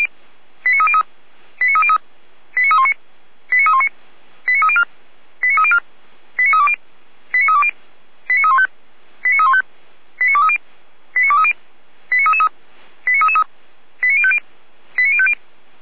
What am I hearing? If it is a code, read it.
Five-tone selective call (ZVEI), heard as quick runs of five tones, tones of about 70 ms. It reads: 89303 89303 89319 89319 89304 89304 89320 89320 89315 89315 89230 89230 89333 89333 89050 89050